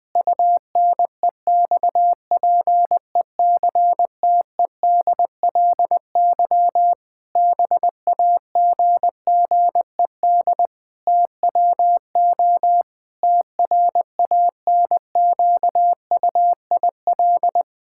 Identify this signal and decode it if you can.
Morse code, heard as a keyed tone, about 20 wpm, 700 Hz. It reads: UNEXPECTEDLY BAGGED TWO TRANQUIL
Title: Pangram Challenge v2 - 20wpm